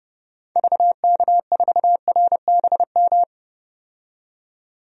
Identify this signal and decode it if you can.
Morse code, heard as a keyed tone, about 31 wpm, 690 Hz. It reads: VK4RBM
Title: DXCC-100 Call Signs - 6 Characters - 30wpm